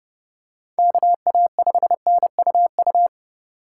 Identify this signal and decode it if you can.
Morse code, heard as a keyed tone, about 27 wpm, 700 Hz. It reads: KA5NUU